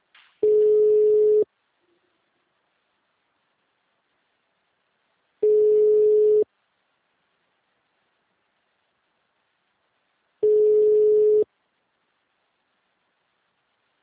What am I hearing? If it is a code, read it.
Ringback tone (Europe, 425 Hz).
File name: out- -2005- - - .268